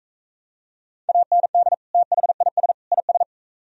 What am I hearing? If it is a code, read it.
Morse code, heard as a keyed tone, about 42 wpm, 700 Hz. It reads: AND THIS IS